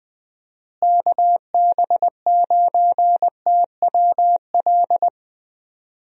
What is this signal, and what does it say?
Morse code, heard as a keyed tone, about 20 wpm, 700 Hz. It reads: KB9TWL
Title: USA Call Signs - 6 Characters - Repeated 2x - 20wpm